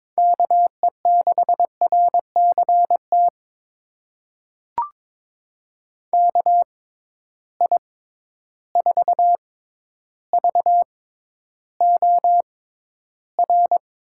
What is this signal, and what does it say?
Morse code, heard as a keyed tone, about 22 wpm, spaced as for 11 wpm, 700 Hz. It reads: KE6RCT KI4VOR
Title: USA Call Signs - 6 Characters - 6x Character Spacing - 22wpm